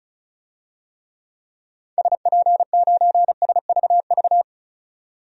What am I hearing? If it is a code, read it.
Morse code, heard as a keyed tone, about 34 wpm, 700 Hz. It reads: SP9SVV
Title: DXCC-100 Call Signs - 6 Characters - 35wpm